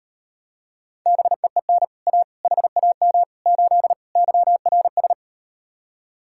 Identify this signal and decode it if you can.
Morse code, heard as a keyed tone, about 38 wpm, 700 Hz. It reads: BEEN A HAM 8 YRS